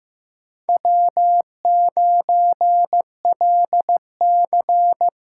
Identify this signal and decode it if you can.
Morse code, heard as a keyed tone, about 15 wpm, 700 Hz. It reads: W9LC
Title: USA Call Signs - 5 Characters - 6x Character Spacing - 15wpm